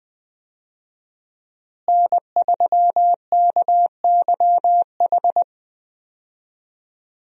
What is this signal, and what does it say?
Morse code, heard as a keyed tone, about 20 wpm, 700 Hz. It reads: N3KYH
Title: USA Call Signs - 5 Characters - Repeated 2x - 20wpm